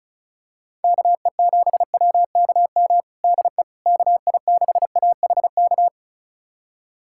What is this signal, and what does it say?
Morse code, heard as a keyed tone, about 36 wpm, 700 Hz. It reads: KE7WKM DE KI6AHK
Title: USA Callsign DE Callsign - 6 Characters - 35wpm